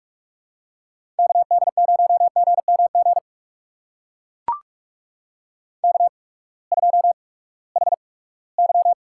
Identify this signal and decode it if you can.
Morse code, heard as a keyed tone, about 44 wpm, spaced as for 22 wpm, 700 Hz. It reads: KD0GMG K2HY
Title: USA Call Signs - 8x Character Spacing - 45wpm